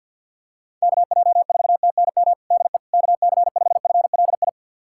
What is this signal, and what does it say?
Morse code, heard as a keyed tone, about 46 wpm, 700 Hz. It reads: KJ4TNM DE KC5FLI
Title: USA Callsign DE Callsign - 6 Characters - 50wpm